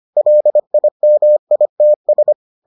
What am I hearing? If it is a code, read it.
Morse code, heard as a keyed tone, about 25 wpm, 600 Hz. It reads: LIMITS